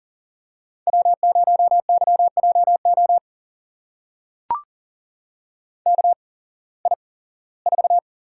Morse code, heard as a keyed tone, about 37 wpm, spaced as for 21 wpm, 700 Hz. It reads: W0YJO KI4
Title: USA Call Signs - 8x Character Spacing - 40wpm